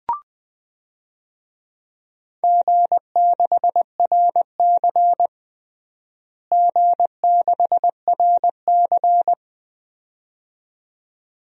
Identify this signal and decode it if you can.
Morse code, heard as a keyed tone, about 20 wpm, 700 Hz. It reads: G6RC G6RC
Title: DXCC-100 Call Signs - 4 Characters - Repeated 2x - 20wpm